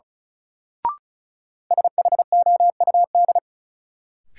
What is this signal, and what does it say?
Morse code, heard as a keyed tone, about 34 wpm, 700 Hz. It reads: SHOUD
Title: Sentences from Top 1000 Words - Review Part 4 - New Word Drill - 35wpm